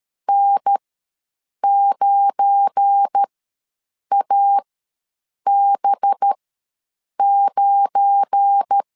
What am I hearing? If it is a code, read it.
Morse code, heard as a keyed tone, about 13 wpm, spaced as for 7 wpm, 800 Hz. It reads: N9AB9
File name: LEZIONI / 7 GF9Z / 7 GF9 ws 12wpm 3344